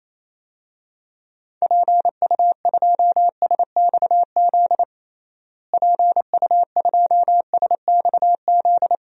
Morse code, heard as a keyed tone, about 27 wpm, 700 Hz. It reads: PU2SXZ PU2SXZ
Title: DXCC-100 Call Signs - 6 Characters - Repeated 2x - 28wpm